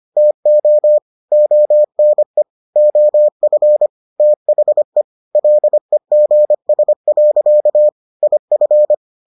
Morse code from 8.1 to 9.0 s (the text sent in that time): IF